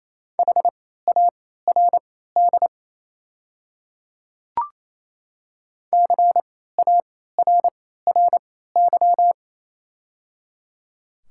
Morse code, heard as a keyed tone, about 28 wpm, spaced as for 16 wpm, 700 Hz. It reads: HARD CARRY
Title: 3x Character Spacing - Top 300 Words - 28wpm